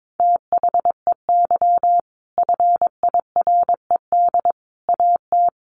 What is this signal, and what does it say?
Morse code, heard as a keyed tone, about 22 wpm, 700 Hz. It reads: THEY FIRED AT
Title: A Sound Of Thunder 22wpm